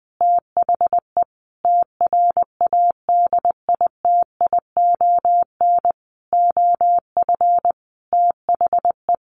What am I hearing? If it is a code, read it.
Morse code, heard as a keyed tone, about 19 wpm, 700 Hz. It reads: THE TRADITION OF THE
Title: Sentences from Top 1500 Words 20wpm